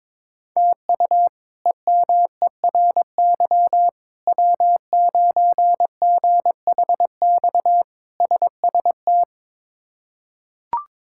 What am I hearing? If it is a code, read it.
Morse code, heard as a keyed tone, about 22 wpm, 700 Hz. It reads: TU EMERY W9GHX SST
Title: K1USN SST QSO Practice - 22wpm